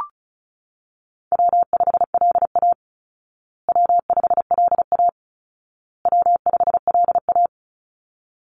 Morse code, heard as a keyed tone, about 34 wpm, 700 Hz. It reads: W5LA W5LA W5LA
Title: Call Signs Repeated 2x - 35wpm